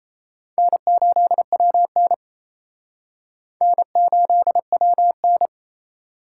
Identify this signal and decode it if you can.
Morse code, heard as a keyed tone, about 34 wpm, 700 Hz. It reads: N8WN N8WN
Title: Speed Racing - USA 1x2 Call Signs 28wpm